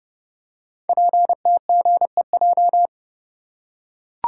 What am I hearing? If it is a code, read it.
Morse code, heard as a keyed tone, about 29 wpm, 700 Hz. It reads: PTGEJ
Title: Code Groups - 5 Letters - 30wpm